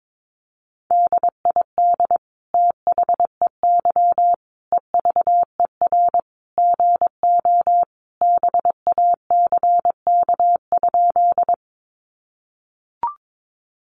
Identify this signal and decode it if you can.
Morse code, heard as a keyed tone, about 22 wpm, 700 Hz. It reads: DID THEY EVER GO BACK?
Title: Sentences from Top 400 Words 22wpm